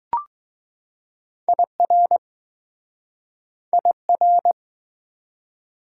Morse code, heard as a keyed tone, about 20 wpm, 700 Hz. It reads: IR IR